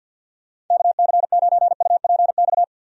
Morse code, heard as a keyed tone, about 52 wpm, 700 Hz. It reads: KC9UPX
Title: USA Call Signs - 6 Characters - Repeated 2x - 50wpm